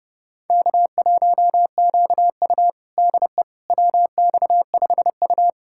Morse code, heard as a keyed tone, about 32 wpm, 700 Hz. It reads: K1QU DE WX5U
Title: USA Callsign DE Callsign - 4 Characters - 30wpm